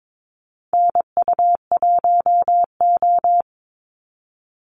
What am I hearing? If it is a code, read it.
Morse code, heard as a keyed tone, about 22 wpm, 700 Hz. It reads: NU1O
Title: Call Signs - 22wpm